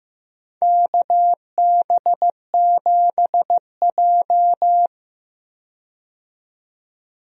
Morse code, heard as a keyed tone, about 15 wpm, 700 Hz. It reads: KB7J
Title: USA Call Signs - 4 Characters - 15wpm